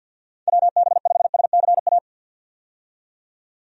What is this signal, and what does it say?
Morse code, heard as a keyed tone, about 50 wpm, 700 Hz. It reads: WB5SCA
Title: USA Call Signs - 6 Characters - 50wpm